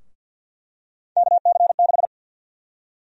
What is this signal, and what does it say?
Morse code, heard as a keyed tone, about 52 wpm, 700 Hz. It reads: KC6
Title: USA Call Sign Prefixes - Repeated 2x - 50wpm